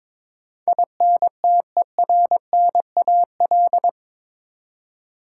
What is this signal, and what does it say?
Morse code, heard as a keyed tone, about 22 wpm, 700 Hz. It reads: INTERNAL
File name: practice / 22wpm / Eight Letter Words - 22wpm